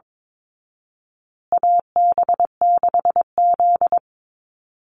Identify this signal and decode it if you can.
Morse code, heard as a keyed tone, about 22 wpm, 700 Hz. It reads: AB6Z
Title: Call Signs Repeated 2x - 22wpm